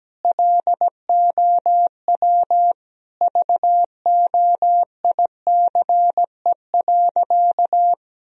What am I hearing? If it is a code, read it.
Morse code, heard as a keyed tone, about 17 wpm, 700 Hz. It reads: LOW VOICE.
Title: Sentences from Top 1000 Words - Review Part 1 - 17wpm